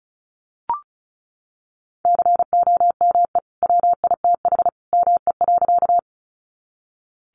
Morse code, heard as a keyed tone, about 34 wpm, 700 Hz. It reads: COME WITH ME.
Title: Sentences from Top 200 Words 35wpm